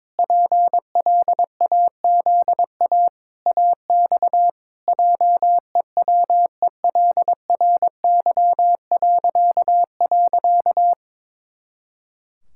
Morse code, heard as a keyed tone, about 22 wpm, 700 Hz. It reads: PLAZA AX JEWELRY..
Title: Pangram Challenge v2 - 22wpm